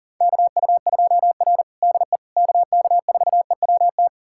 Morse code, heard as a keyed tone, about 40 wpm, 700 Hz. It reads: KU2R DE KK4EWT
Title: USA Callsign DE Callsign 40wpm